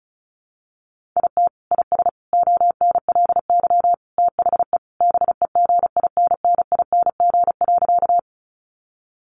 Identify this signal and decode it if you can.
Morse code, heard as a keyed tone, about 34 wpm, 700 Hz. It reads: IT IS ONLY THE BEGINNING.
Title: Sentences from Top 800 Words 35wpm